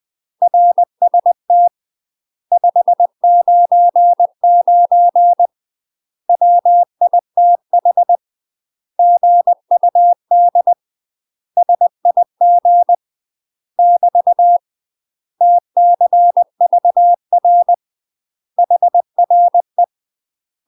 Morse code, heard as a keyed tone, about 20 wpm, 700 Hz. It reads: RST 599 WITH GUD SIG <BT> TCVR HRE
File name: UK MASTER QSOs TRAINING / 20WPM / UK MASTER QSOs 0029